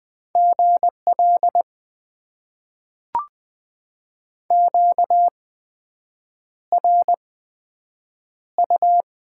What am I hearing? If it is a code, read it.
Morse code, heard as a keyed tone, about 20 wpm, spaced as for 7 wpm, 700 Hz. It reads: GL QRU
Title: Top 90 QSO Elements - 8x Character Spacing - 20wpm